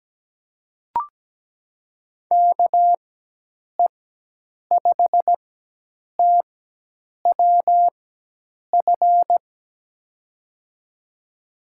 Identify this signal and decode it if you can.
Morse code, heard as a keyed tone, about 17 wpm, spaced as for 8 wpm, 700 Hz. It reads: KE5TWF
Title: USA Call Signs - 6 Characters - 4x Character Spacing - 17wpm